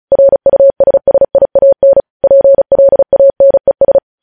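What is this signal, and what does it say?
Morse code, heard as a keyed tone, about 36 wpm, 550 Hz. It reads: RUSSIAN PLANES
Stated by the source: News headlines sent in Morse code at 35 WPM.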